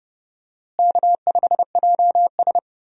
Morse code, heard as a keyed tone, about 31 wpm, 700 Hz. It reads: K5JS
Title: USA Call Signs - 4 Characters - Repeated 3x - 30wpm